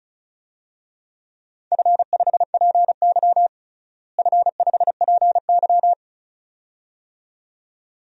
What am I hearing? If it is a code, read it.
Morse code, heard as a keyed tone, about 36 wpm, 700 Hz. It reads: F5PY F5PY
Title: DXCC-100 Call Signs - 4 Characters - Repeated 2x - 35wpm